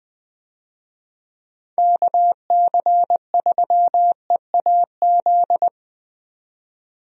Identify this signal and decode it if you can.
Morse code, heard as a keyed tone, about 20 wpm, 700 Hz. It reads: KC3EAZ
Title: USA Call Signs - 6 Characters - 20wpm